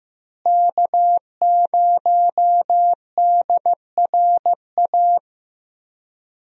Morse code, heard as a keyed tone, about 15 wpm, 700 Hz. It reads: K0DRA
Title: USA Call Signs - 5 Characters - Repeated 2x - 15wpm